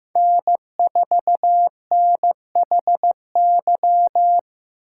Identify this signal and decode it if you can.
Morse code, heard as a keyed tone, about 18 wpm, 700 Hz. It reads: N4NHY
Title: Speed Racing - USA 1x3 Call Signs 15wpm